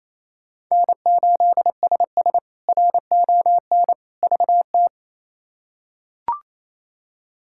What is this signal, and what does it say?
Morse code, heard as a keyed tone, about 28 wpm, 700 Hz. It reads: N8SS RON VT
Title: K1USN SST QSO Practice - 28wpm